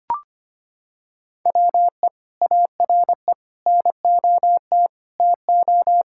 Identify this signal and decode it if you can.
Morse code, heard as a keyed tone, about 25 wpm, 700 Hz. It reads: WE ARE NOT TO
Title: Sentences from Top 1600 Words 25wpm